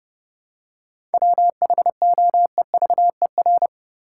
Morse code, heard as a keyed tone, about 32 wpm, 700 Hz. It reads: WHOEVER
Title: Seven Letter Words - 30wpm